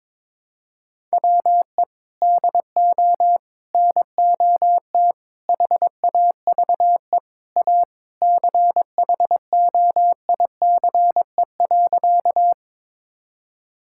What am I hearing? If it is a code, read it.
Morse code, heard as a keyed tone, about 21 wpm, 700 Hz. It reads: WE DO NOT HAVE A CHOICE.
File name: practice / 22wpm / Sentences with 3-Word Phrases - 22wpm